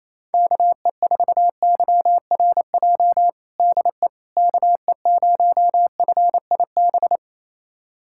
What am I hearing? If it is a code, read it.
Morse code, heard as a keyed tone, about 28 wpm, 700 Hz. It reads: KE4YRJ DE KE0FIB
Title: USA Callsign DE Callsign - 6 Characters - 28wpm